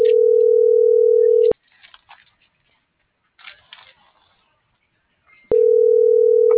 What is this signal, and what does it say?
Ringback tone (North America), 440 + 480 Hz.
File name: CENTRAL SOTOVA / out alejandro.rodriguez